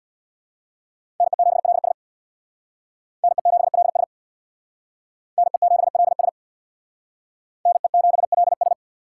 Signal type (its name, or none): Morse code